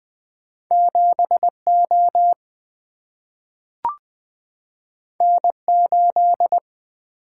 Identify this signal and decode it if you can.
Morse code, heard as a keyed tone, about 20 wpm, 700 Hz. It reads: 7 O N 8
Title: Single-Letter Single-Number 20wpm